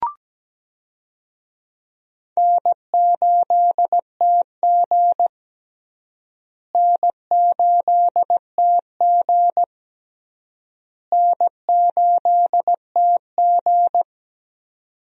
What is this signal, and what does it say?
Morse code, heard as a keyed tone, about 17 wpm, 700 Hz. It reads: N8TG N8TG N8TG
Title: USA Call Signs - 4 Characters - Repeated 3x - 17wpm